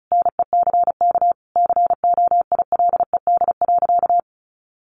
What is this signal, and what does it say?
Morse code, heard as a keyed tone, about 34 wpm, 700 Hz. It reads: NECK COILED.
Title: A Sound Of Thunder 35wpm